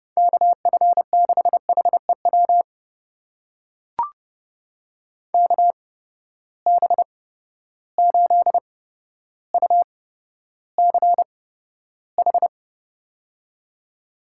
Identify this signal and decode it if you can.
Morse code, heard as a keyed tone, about 31 wpm, spaced as for 13 wpm, 700 Hz. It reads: KF6HEW KB8UCH
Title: USA Call Signs - 6 Characters - 8x Character Spacing - 30wpm